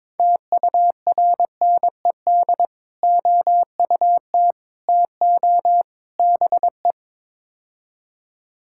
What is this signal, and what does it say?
Morse code, heard as a keyed tone, about 21 wpm, 700 Hz. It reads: TURNED OUT TO BE
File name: practice / 22wpm / Build A Phrase - 5-Word Phrases - 22wpm